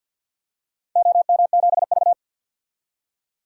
Morse code, heard as a keyed tone, about 53 wpm, 700 Hz. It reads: OM7V
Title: DXCC-100 Call Signs - 4 Characters - 50wpm